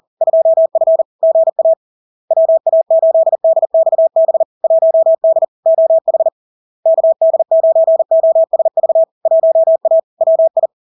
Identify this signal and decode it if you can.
Morse code, heard as a keyed tone, about 40 wpm, 650 Hz. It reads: 2F GA WA8DXB 1D OH KD9OSV 1A WI
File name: LARK-FD-2022-Log-in-CW-40wpm-650-Hz